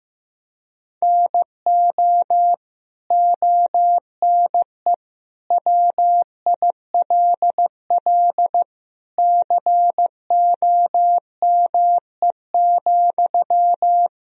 Morse code, heard as a keyed tone, about 15 wpm, 700 Hz. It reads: NO ONE WILL COME,
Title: Sentences from Top 100 Words and Abbreviations 15wpm